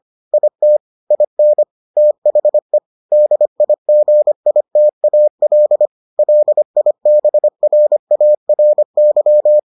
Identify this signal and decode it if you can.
Morse code, heard as a keyed tone, about 25 wpm, 600 Hz. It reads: IT IN THE DIGITAL LIBRARY